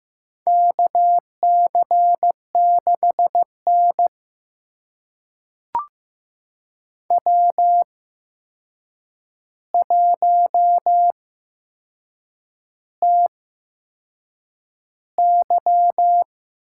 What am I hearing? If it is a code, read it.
Morse code, heard as a keyed tone, about 15 wpm, spaced as for 6 wpm, 700 Hz. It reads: KC6N W1TY
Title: USA Call Signs - 4 Characters - 8x Character Spacing - 15wpm